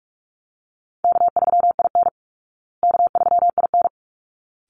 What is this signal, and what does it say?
Morse code, heard as a keyed tone, about 44 wpm, 700 Hz. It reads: K3IN K3IN
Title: Easy Call Signs Repeated 2x - 45wpm